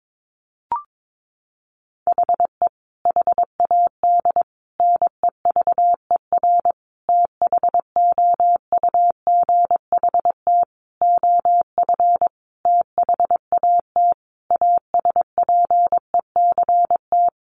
Morse code, heard as a keyed tone, about 22 wpm, 700 Hz. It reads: HE HAD NEVER THOUGHT OF THAT ASPECT
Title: Sentences from Top 1300 Words 22wpm